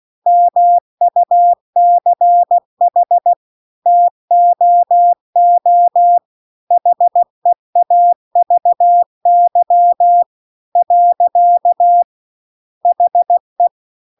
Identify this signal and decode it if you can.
Morse code, heard as a keyed tone, about 16 wpm, 700 Hz. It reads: MUCH TOO HEAVY . HE